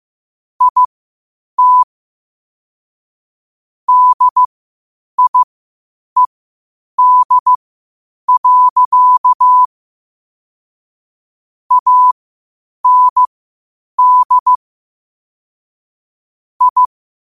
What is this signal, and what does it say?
Morse code, heard as a keyed tone, about 15 wpm, spaced as for 8 wpm, 1000 Hz. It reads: IT DIED. AND I
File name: qotdfiles / 8wpm